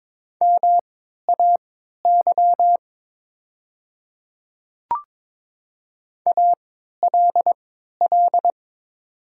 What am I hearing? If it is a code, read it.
Morse code, heard as a keyed tone, about 22 wpm, spaced as for 12 wpm, 700 Hz. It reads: MAY ALL
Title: 3x Character Spacing - Top 100 Words - 22wpm